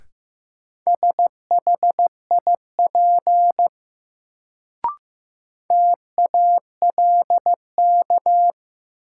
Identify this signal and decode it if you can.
Morse code, heard as a keyed tone, about 15 wpm, 700 Hz. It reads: SHIP TALK
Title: Sentences from Top 400 Words - New Word Drill - 15wpm